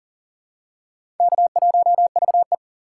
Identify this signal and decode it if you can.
Morse code, heard as a keyed tone, about 38 wpm, 700 Hz. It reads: K1VE
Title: USA Call Signs - 4 Characters - 6x Character Spacing - 40wpm